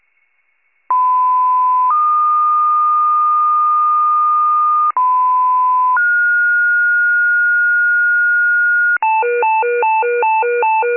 Two-tone page, heard as a one-second tone, then a three-second tone, about 1010 then 1220 Hz, about 1010 then 1430 Hz, with noise. It is heard from 0.9 to 9.0 s.